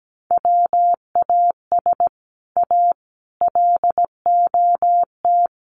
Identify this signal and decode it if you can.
Morse code, heard as a keyed tone, about 17 wpm, 700 Hz. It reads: WAS A LOT
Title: Sets of 4 Words 17wpm